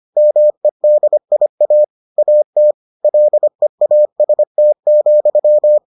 Morse code, heard as a keyed tone, about 25 wpm, 600 Hz. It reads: MEDIA AT LEAST,